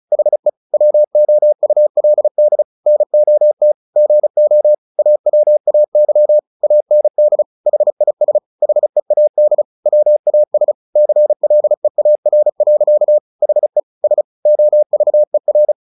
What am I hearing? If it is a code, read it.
Morse code, heard as a keyed tone, about 35 wpm, 600 Hz. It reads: HE WOULD NOT GO AWAY AND HIS HEAD WAS CLEAR. HE S OVER
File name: the old man and the sea 2 35wpm